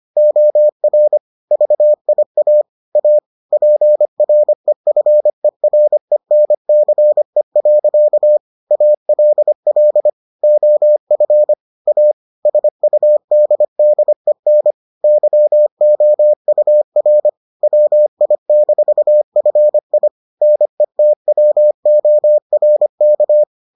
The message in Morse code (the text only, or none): OR VIA A PREFERENCE. ALL OF A SUDDEN YOUR WI-FI NETWORK